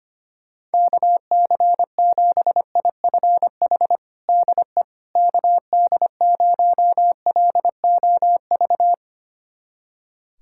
Morse code, heard as a keyed tone, about 25 wpm, 700 Hz. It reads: KC7IFH DE KD0LOV
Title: USA Callsign DE Callsign 25wpm